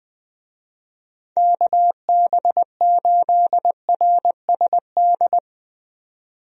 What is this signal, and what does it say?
Morse code, heard as a keyed tone, about 20 wpm, 700 Hz. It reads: KB8RSD
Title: USA Call Signs - 6 Characters - 20wpm